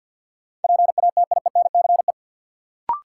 Morse code, heard as a keyed tone, about 53 wpm, 700 Hz. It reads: PATIENCE
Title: Sentences from Top 1300 Words - New Word Drill Alt - 50wpm